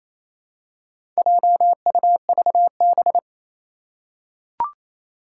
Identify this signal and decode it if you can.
Morse code, heard as a keyed tone, about 28 wpm, 700 Hz. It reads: JUVB
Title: Four Challenging Letter-Number Combinations 28wpm